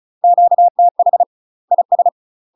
Morse code, heard as a keyed tone, about 34 wpm, 700 Hz. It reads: QTH IS
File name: UK MASTER QSOs TRAINING / 35WPM / UK MASTER QSOs 0005